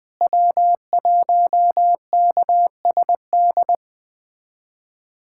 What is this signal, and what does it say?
Morse code, heard as a keyed tone, about 20 wpm, 700 Hz. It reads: W1KSD
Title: USA Callsign DE Callsign 20wpm